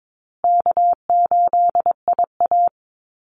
Morse code, heard as a keyed tone, about 22 wpm, 700 Hz. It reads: K8IA
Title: Easy Call Signs Repeated 2x - 22wpm